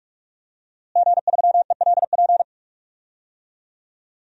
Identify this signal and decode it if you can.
Morse code, heard as a keyed tone, about 44 wpm, 700 Hz. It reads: G3ELP
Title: DXCC-100 Call Signs - 5 Characters - Repeated 3x - 45wpm